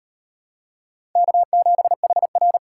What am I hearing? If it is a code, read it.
Morse code, heard as a keyed tone, about 38 wpm, 700 Hz. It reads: K7HR